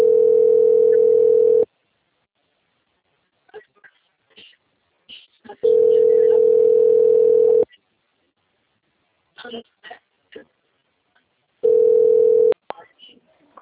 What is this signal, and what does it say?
Ringback tone (North America), 440 + 480 Hz, with speech.
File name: out- - - - - .20550